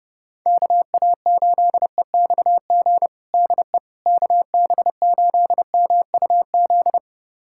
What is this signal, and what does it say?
Morse code, heard as a keyed tone, about 30 wpm, 700 Hz. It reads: KA8EXG DE KB8MUZ
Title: USA Callsign DE Callsign 30wpm